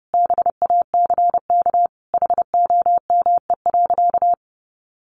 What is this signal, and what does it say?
Morse code, heard as a keyed tone, about 28 wpm, 700 Hz. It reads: BACK HOME.
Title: Sentences from Top 500 Words 30wpm